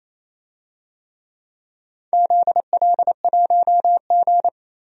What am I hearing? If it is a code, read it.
Morse code, heard as a keyed tone, about 28 wpm, 700 Hz. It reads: ZL1G
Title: DXCC-100 Call Signs - 4 Characters - 28wpm